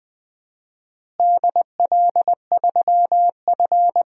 Morse code, heard as a keyed tone, about 20 wpm, 700 Hz. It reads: DL3F
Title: DXCC-100 Call Signs - 4 Characters - Repeated 3x - 20wpm